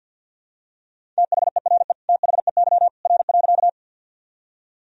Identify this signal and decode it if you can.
Morse code, heard as a keyed tone, about 52 wpm, 700 Hz. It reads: THERE THEY R.
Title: Sentences from Top 100 Words and Abbreviations 50wpm